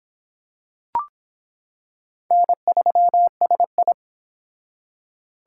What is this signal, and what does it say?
Morse code, heard as a keyed tone, about 25 wpm, 700 Hz. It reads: N3SI